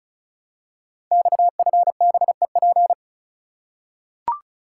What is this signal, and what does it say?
Morse code, heard as a keyed tone, about 34 wpm, 700 Hz. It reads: XFBEP
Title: Code Groups - 5 Letter:Numbers - 35wpm